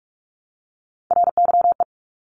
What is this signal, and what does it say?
Morse code, heard as a keyed tone, about 46 wpm, 700 Hz. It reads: RYE
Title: Multi-letter Phonemes Repeated 3x 45wpm